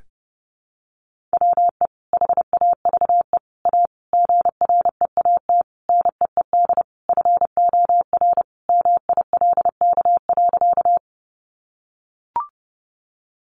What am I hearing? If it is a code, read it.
Morse code, heard as a keyed tone, about 28 wpm, 700 Hz. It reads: WE HAVE A GREAT NEED FOR MILK.
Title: Sentences from Top 1500 Words 30wpm